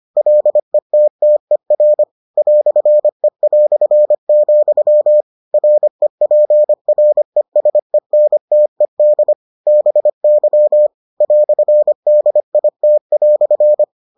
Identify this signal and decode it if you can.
Morse code, heard as a keyed tone, about 25 wpm, 600 Hz. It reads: LETTER "E", REPRESENTED BY "DIT"